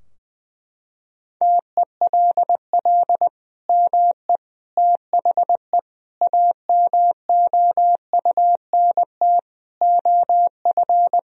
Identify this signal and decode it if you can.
Morse code, heard as a keyed tone, about 20 wpm, 700 Hz. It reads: TELL ME THE AMOUNT OF
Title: Sentences from Top 1000 Words - Review Part 4 - 20wpm